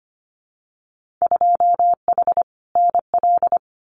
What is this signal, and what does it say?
Morse code, heard as a keyed tone, about 25 wpm, 700 Hz. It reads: 2H NL
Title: Winter Field Day Exchanges Encore 25wpm